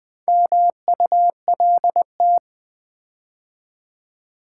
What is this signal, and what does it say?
Morse code, heard as a keyed tone, about 20 wpm, 700 Hz. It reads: MULT
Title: Build A Word - 4 Letters At A Time - 8-Letter Words - 20wpm